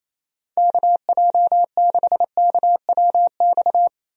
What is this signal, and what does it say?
Morse code, heard as a keyed tone, about 28 wpm, 700 Hz. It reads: KJ6KWX
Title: USA Call Signs - 6 Characters - 28wpm